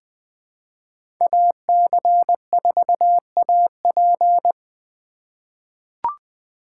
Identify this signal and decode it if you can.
Morse code, heard as a keyed tone, about 20 wpm, 700 Hz. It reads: AC4AP
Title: USA Call Signs - 2x Character Spacing - 20wpm